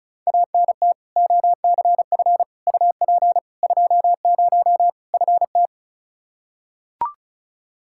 Morse code, heard as a keyed tone, about 36 wpm, 700 Hz. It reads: ANT OCF UP 20 FT
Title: K1USN SST QSO Practice 35wpm